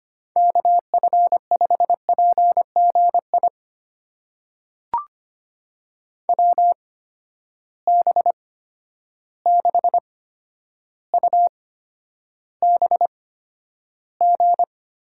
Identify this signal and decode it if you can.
Morse code, heard as a keyed tone, about 25 wpm, spaced as for 11 wpm, 700 Hz. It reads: KF5PGI WB6UBG
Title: USA Call Signs - 6 Characters - 8x Character Spacing - 25wpm